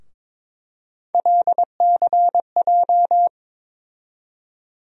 Morse code, heard as a keyed tone, about 22 wpm, 700 Hz. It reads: LCJ
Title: USA Call Sign Postfixes - 3 Letters 22wpm